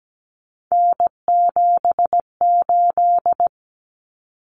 Morse code, heard as a keyed tone, about 17 wpm, 700 Hz. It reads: N78
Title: Common Cut Numbers - 3-Digit 17wpm